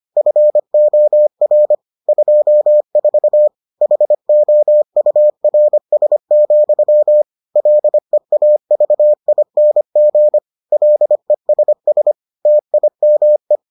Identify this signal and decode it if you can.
Morse code, heard as a keyed tone, about 25 wpm, 600 Hz. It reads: FOR 24 HOURS, LEAVING LESS TIME